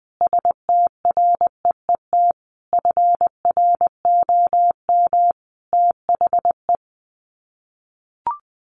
Morse code, heard as a keyed tone, about 20 wpm, 700 Hz. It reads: STREET FROM THE
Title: Sets of 5 Words Encore 20wpm